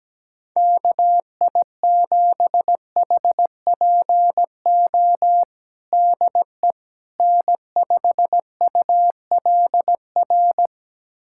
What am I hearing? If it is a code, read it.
Morse code, heard as a keyed tone, about 17 wpm, 700 Hz. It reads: KI7HPO DE N5ULR
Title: USA Callsign DE Callsign 17wpm